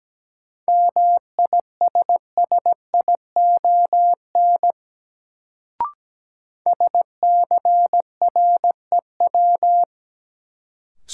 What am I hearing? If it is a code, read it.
Morse code, heard as a keyed tone, about 17 wpm, 700 Hz. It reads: MISSION SCREW
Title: Sentences from Top 1600 Words - New Word Drill - 17wpm